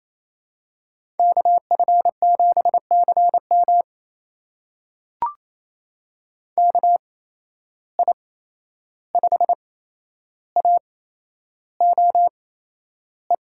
Morse code, heard as a keyed tone, about 28 wpm, spaced as for 11 wpm, 700 Hz. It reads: KF7CM KI5AOE
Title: USA Call Signs - 8x Character Spacing - 28wpm